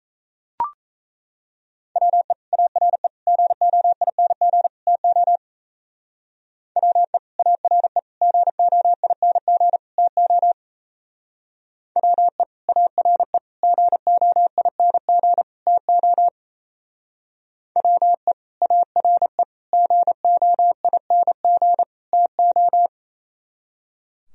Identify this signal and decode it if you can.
Morse code, heard as a keyed tone, about 34 wpm, 700 Hz. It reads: WE ARE GOING TO WE ARE GOING TO WE ARE GOING TO WE ARE GOING TO
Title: Speed-Racing - Sets of 4 Words 28wpm